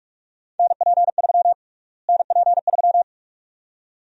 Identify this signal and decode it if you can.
Morse code, heard as a keyed tone, about 41 wpm, 700 Hz. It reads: NP3 NP3
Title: USA Call Sign Prefixes - Repeated 2x - 45wpm